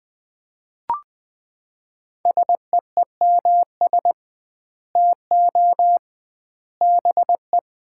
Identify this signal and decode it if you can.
Morse code, heard as a keyed tone, about 20 wpm, 700 Hz. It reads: SEEMS TO BE
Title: 2x Word-Spacing - Sets of 3 Words - 20wpm